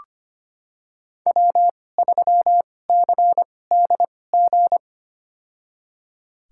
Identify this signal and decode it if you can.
Morse code, heard as a keyed tone, about 25 wpm, spaced as for 18 wpm, 700 Hz. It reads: W3CDG
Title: USA Call Signs - 5 Characters - 2x Character Spacing - 25wpm